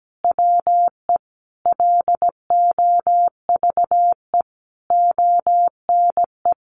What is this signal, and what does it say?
Morse code, heard as a keyed tone, about 17 wpm, 700 Hz. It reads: WE LOVE ONE
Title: Sentences from Top 600 Words 17wpm